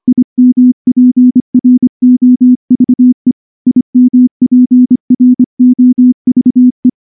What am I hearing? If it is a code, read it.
Morse code, heard as a keyed tone, about 24 wpm, 250 Hz. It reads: IMPROVE IMPROVE
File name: morse-training-25WPM